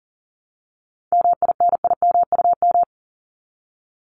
Morse code, heard as a keyed tone, about 39 wpm, 700 Hz. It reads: MINIMUM
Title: Top 1251-1500 Words 40wpm